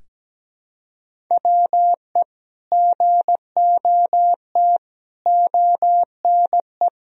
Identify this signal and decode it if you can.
Morse code, heard as a keyed tone, about 17 wpm, 700 Hz. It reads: WE GOT ONE
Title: Sentences from Top 300 Words and Abbreviations 17wpm